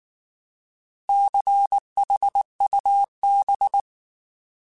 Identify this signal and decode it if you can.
Morse code, heard as a keyed tone, about 19 wpm, 800 Hz. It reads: CHUB